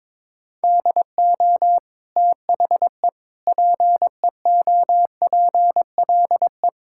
Morse code, heard as a keyed tone, about 21 wpm, 700 Hz. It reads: DO THE PEOPLE
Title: Sentences from Top 1000 Words - Review Part 5 - 22wpm